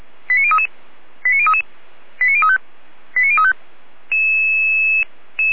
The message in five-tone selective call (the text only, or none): 89033 89033 89035 89035